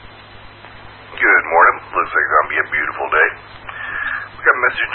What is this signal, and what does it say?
Emergency Alert System (SAME) tones, heard as harsh buzzing bursts, with speech. It is heard from 1.1 to 4.4 s.